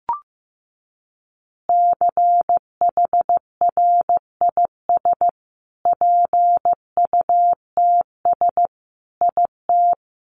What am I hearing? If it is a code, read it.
Morse code, heard as a keyed tone, about 15 wpm, 700 Hz. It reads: CHRIS PUTS IT
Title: Easy Phrases 15wpm